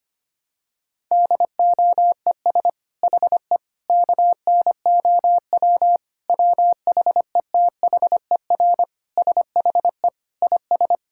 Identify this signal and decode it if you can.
Morse code, heard as a keyed tone, about 25 wpm, 700 Hz. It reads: DOES HE KNOW WHETHER SHE IS
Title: Sentences from Top 1000 Words - Review Part 6 - 25wpm